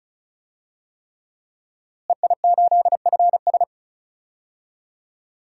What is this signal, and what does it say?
Morse code, heard as a keyed tone, about 34 wpm, 700 Hz. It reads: EI8FS
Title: DXCC-100 Call Signs - 5 Characters - Repeated 3x - 35wpm